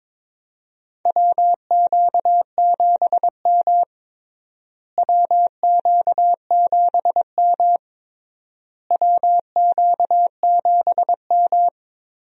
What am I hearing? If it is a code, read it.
Morse code, heard as a keyed tone, about 22 wpm, 700 Hz. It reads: WQ7M WQ7M WQ7M
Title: USA Call Signs - 4 Characters - Repeated 3x - 22wpm